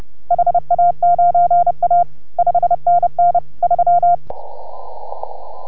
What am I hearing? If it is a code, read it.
Morse code, heard as a keyed tone, about 30 wpm, 690 Hz. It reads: HA9A 5NN 3